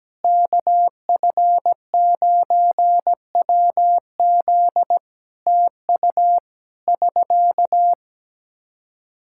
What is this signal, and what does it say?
Morse code, heard as a keyed tone, about 17 wpm, 700 Hz. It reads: KF9WZ TU <SK>
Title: Practice QSO - Exchange 4 - Ending - 17wpm